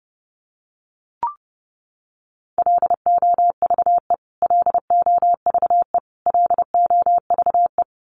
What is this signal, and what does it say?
Morse code, heard as a keyed tone, about 30 wpm, 700 Hz. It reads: LOVE LOVE LOVE
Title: Top 500 Words Repeated 30wpm